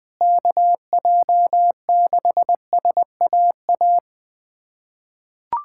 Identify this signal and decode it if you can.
Morse code, heard as a keyed tone, about 21 wpm, 700 Hz. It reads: KJ6SAA
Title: USA Callsign DE Callsign 20wpm